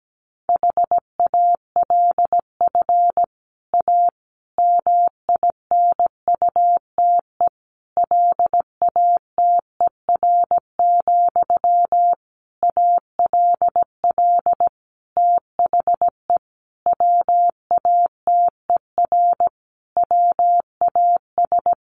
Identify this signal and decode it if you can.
Morse code, heard as a keyed tone, about 17 wpm, 700 Hz. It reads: HALF A MINUTE LATER, ALL THE WATER WAS
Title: Sentences from Top 1100 Words 17wpm